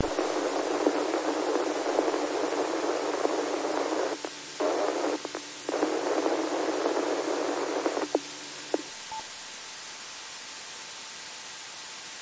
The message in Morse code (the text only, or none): none